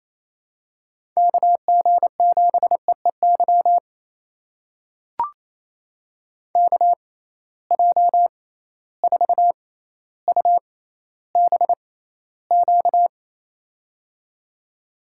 Morse code, heard as a keyed tone, about 28 wpm, spaced as for 14 wpm, 700 Hz. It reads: KG7EEY KJ4UBQ
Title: USA Call Signs - 6 Characters - 6x Character Spacing - 28wpm